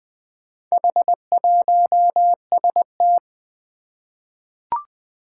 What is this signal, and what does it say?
Morse code, heard as a keyed tone, about 20 wpm, 700 Hz. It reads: H1ST
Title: Four Challenging Letter-Number Combinations 20wpm